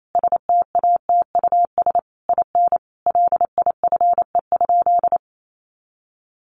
Morse code, heard as a keyed tone, about 28 wpm, 700 Hz. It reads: STATUS IN LIFE?
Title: Sentences from Top 1400 Words 28wpm